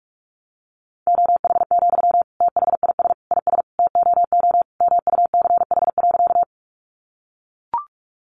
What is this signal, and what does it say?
Morse code, heard as a keyed tone, about 44 wpm, 700 Hz. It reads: OH, THIS IS TOO MUCH.
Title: Sentences from Top 500 Words 45wpm